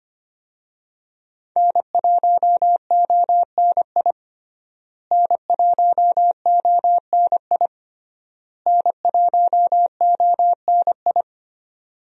Morse code, heard as a keyed tone, about 25 wpm, 700 Hz. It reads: N1ONI N1ONI N1ONI
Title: USA Call Signs - 5 Characters - Repeated 3x - 25wpm